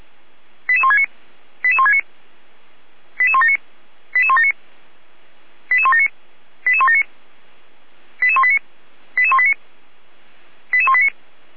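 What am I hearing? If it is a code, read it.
Five-tone selective call (ZVEI), heard as quick runs of five tones, tones of about 70 ms. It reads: 88169 88169 88179 88179 88179 88179 88189 88189 88189
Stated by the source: ZVEI five-tone call 88159, received on SDR.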